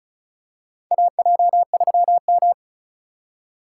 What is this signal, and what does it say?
Morse code, heard as a keyed tone, about 34 wpm, 700 Hz. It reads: AJ3M
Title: USA Call Signs - 4 Characters - 3x Character Spacing - 35wpm